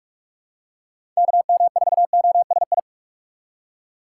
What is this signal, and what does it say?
Morse code, heard as a keyed tone, about 46 wpm, 700 Hz. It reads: KM4OSI
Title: USA Call Signs - 6x Character Spacing - 45wpm